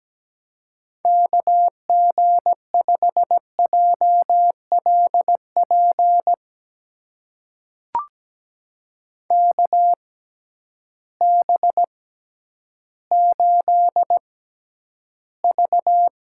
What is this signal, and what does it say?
Morse code, heard as a keyed tone, about 17 wpm, spaced as for 10 wpm, 700 Hz. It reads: KG5JLP KB8V
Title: USA Call Signs - 6 Characters - 6x Character Spacing - 17wpm